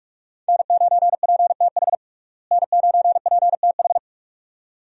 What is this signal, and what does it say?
Morse code, heard as a keyed tone, about 43 wpm, 700 Hz. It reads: N9PTH N9PTH
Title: USA Call Signs - 5 Characters - Repeated 2x - 45wpm